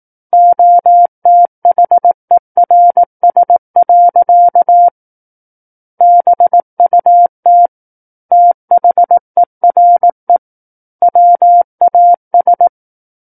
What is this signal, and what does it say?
Morse code, heard as a keyed tone, about 18 wpm, 700 Hz. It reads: OTHERS. BUT THERE WAS
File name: 18 WPM